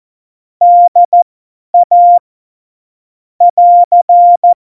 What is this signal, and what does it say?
Morse code, heard as a keyed tone, about 14 wpm, spaced as for 10 wpm, 700 Hz. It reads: DA <AR>
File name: Morsekurs / Witze / Witz 26 Mai 2020, 50 F70